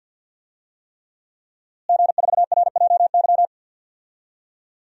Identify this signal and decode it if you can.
Morse code, heard as a keyed tone, about 51 wpm, 700 Hz. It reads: G4RJY
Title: DXCC-100 Call Signs - 5 Characters - Repeated 3x - 50wpm